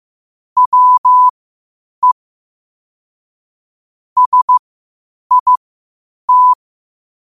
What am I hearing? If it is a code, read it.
Morse code, heard as a keyed tone, about 15 wpm, spaced as for 8 wpm, 1000 Hz. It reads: WE SIT
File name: qotdfiles / 8wpm